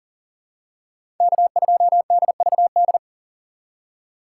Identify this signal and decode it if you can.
Morse code, heard as a keyed tone, about 40 wpm, 700 Hz. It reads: K2DVD
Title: USA Call Signs - 5 Characters - 6x Character Spacing - 40wpm